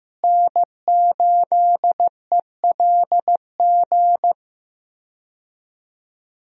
Morse code, heard as a keyed tone, about 15 wpm, 700 Hz. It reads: N8ELG
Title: USA Call Signs - 5 Characters - Repeated 3x - 15wpm